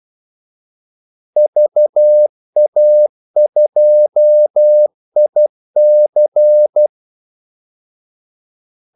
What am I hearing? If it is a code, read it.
Morse code, heard as a keyed tone, about 12 wpm, 600 Hz. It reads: VA2IC
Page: 2 ekstra ordmellemrum mellem kaldesignalerne